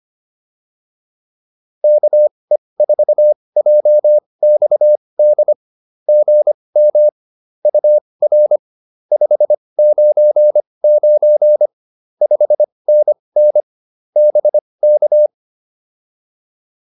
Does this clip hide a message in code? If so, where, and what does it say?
Yes: Morse code from 1.8 to 15.3 s, heard as a keyed tone, about 25 wpm, spaced as for 20 wpm, 600 Hz. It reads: KE4JXD GM UR 599 5NN BK